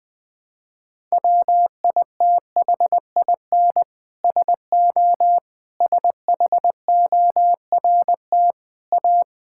Morse code, heard as a keyed tone, about 20 wpm, 700 Hz. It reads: WITHIN SO SHORT A
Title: Sentences from Top 1000 Words - Review Part 5 - 20wpm